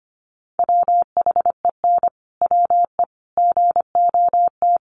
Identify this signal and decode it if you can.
Morse code, heard as a keyed tone, about 25 wpm, 700 Hz. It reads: WHEN WE GOT
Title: Sentences from Top 1400 Words 25wpm